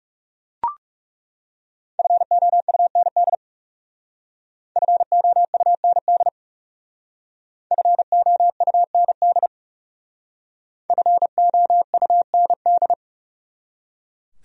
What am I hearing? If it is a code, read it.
Morse code, heard as a keyed tone, about 38 wpm, 700 Hz. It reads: FOUND FOUND FOUND FOUND